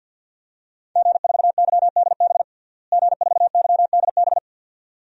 Morse code, heard as a keyed tone, about 47 wpm, 700 Hz. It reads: G4YDB G4YDB
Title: DXCC-100 Call Signs - 5 Characters - Repeated 2x - 50wpm